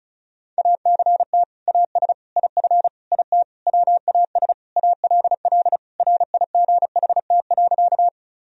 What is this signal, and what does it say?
Morse code, heard as a keyed tone, about 34 wpm, 700 Hz. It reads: ACT AS IF IT WAS ALL RIGHT.
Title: Sentences from Top 1000 Words - Review Part 1 - 35wpm